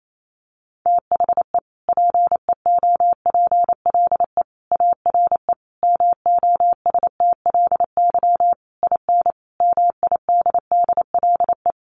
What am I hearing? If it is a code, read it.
Morse code, heard as a keyed tone, about 28 wpm, 700 Hz. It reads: THE PEOPLE ARE MOSTLY IN MIDDLE
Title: Sentences from Top 1300 Words 28wpm